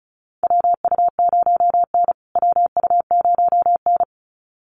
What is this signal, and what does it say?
Morse code, heard as a keyed tone, about 34 wpm, 700 Hz. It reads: WU0N WU0N
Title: USA Call Signs Repeated 2x 35wpm